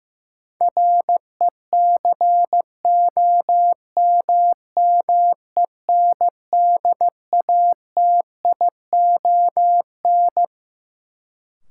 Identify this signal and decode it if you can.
Morse code, heard as a keyed tone, about 15 wpm, 700 Hz. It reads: RECOMMENDATION
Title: Sentences from Top 1400 Words - New Word Drill Alt - 15wpm